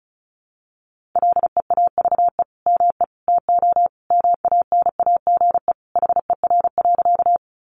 Morse code, heard as a keyed tone, about 34 wpm, 700 Hz. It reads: LEAVE ME TO MANAGE HER.
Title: Sentences from Top 1400 Words 35wpm